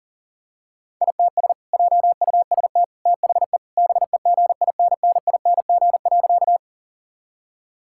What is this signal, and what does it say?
Morse code, heard as a keyed tone, about 40 wpm, 700 Hz. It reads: ITS JUST THE BEGINNING.
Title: Sentences from Top 1000 Words - Review Part 5 - 40wpm